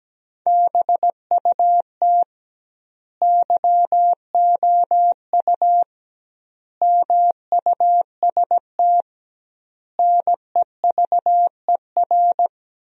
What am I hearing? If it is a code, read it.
Morse code, heard as a keyed tone, about 17 wpm, 700 Hz. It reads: BUT YOU MUST NEVER
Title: 2x Word-Spacing - Sentences from Top 300 Words - 17wpm